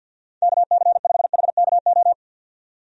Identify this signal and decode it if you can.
Morse code, heard as a keyed tone, about 51 wpm, 700 Hz. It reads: KC5HKO
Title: USA Call Signs - 6 Characters - 50wpm